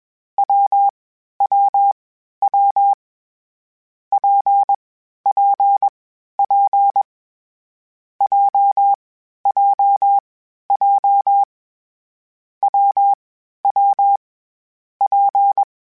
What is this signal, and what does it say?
Morse code, heard as a keyed tone, about 21 wpm, 800 Hz. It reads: W W W P P P J J J W W P
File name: Variable Speed / Variable Speed-mp3s / WPJ WPJ1 sp @20